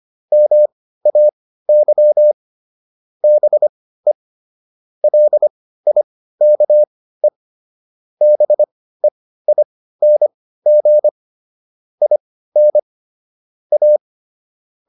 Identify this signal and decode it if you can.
Morse code, heard as a keyed tone, about 25 wpm, spaced as for 15 wpm, 600 Hz. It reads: MAY BE LIKE BEING IN A